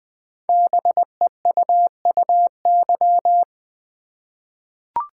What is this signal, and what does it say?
Morse code, heard as a keyed tone, about 20 wpm, 700 Hz. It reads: BEUUY
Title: Code Groups - 5 Letters - 20wpm